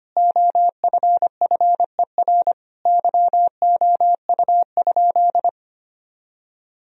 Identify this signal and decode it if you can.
Morse code, heard as a keyed tone, about 25 wpm, 700 Hz. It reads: OFFER YOU?
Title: Sentences from Top 1000 Words - Review Part 4 - 25wpm